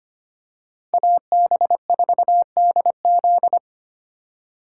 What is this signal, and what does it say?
Morse code, heard as a keyed tone, about 25 wpm, 700 Hz. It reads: AB4DZ
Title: USA Call Signs - 5 Characters - 2x Character Spacing - 25wpm